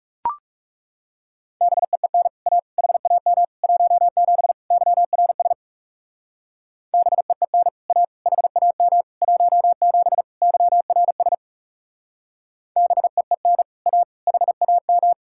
Morse code, heard as a keyed tone, about 36 wpm, 700 Hz. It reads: BEEN A HAM 17 YRS BEEN A HAM 17 YRS BEEN A HAM
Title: Speed-Racing Common QSO Phrases Encore - 30wpm